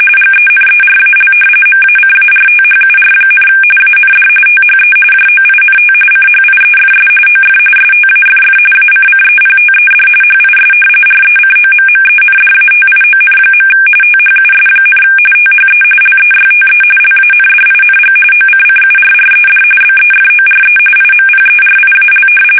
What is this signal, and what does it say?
RTTY, about 45.45 baud. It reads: GENERAL MACARTHUR, AND THE JOINT CHIEFS OF STAFF ACCEPTED HIS RECOMMENDATION. THUS, THE LEYTE LANDING OCCURRED BEFORE IMPERIAL
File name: fsk850 / battleshipnj